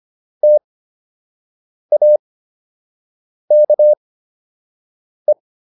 Morse code, heard as a keyed tone, about 25 wpm, spaced as for 6 wpm, 600 Hz. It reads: TAKE